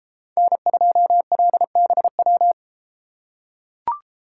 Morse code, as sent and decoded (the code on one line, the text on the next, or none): -. ..--- .-.. -... .--
N2LBW